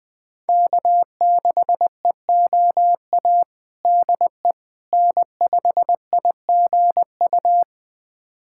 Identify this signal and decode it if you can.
Morse code, heard as a keyed tone, about 20 wpm, 700 Hz. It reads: K6EOA DE N5IGU
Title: USA Callsign DE Callsign - 5 Characters - 20wpm